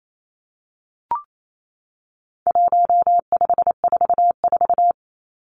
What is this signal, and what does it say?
Morse code, heard as a keyed tone, about 28 wpm, 700 Hz. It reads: 1544
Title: Numbers - 4-Digit 28wpm